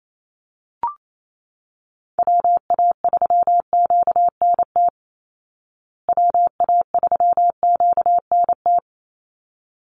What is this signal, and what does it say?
Morse code, heard as a keyed tone, about 28 wpm, 700 Hz. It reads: WA3QNT WA3QNT
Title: Call Signs Repeated 3x - 28wpm